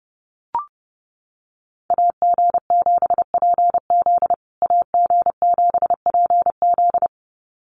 Morse code, heard as a keyed tone, about 28 wpm, 700 Hz. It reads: AG7PZ AG7PZ
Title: USA Call Signs Repeated 2x 30wpm